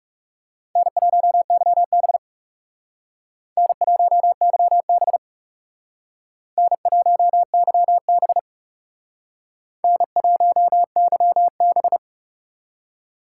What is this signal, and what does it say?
Morse code, heard as a keyed tone, about 41 wpm, 700 Hz. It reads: N1YB N1YB N1YB N1YB